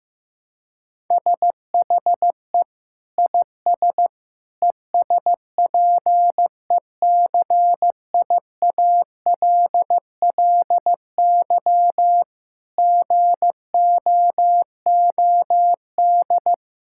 Morse code, heard as a keyed tone, about 15 wpm, 700 Hz. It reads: SHE IS ESPECIALLY GOOD
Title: Sentences from Top 1000 Words - Review Part 6 - 15wpm